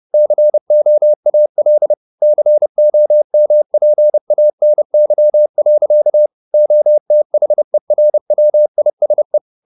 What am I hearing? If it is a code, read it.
Morse code, heard as a keyed tone, about 30 wpm, 600 Hz. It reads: COAL COMPANY. OTHERWISE